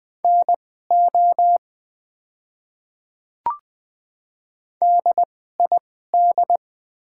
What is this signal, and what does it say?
Morse code, heard as a keyed tone, about 20 wpm, spaced as for 15 wpm, 700 Hz. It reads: NO DID
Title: 2x Character Spacing - Top 100 Words - 20wpm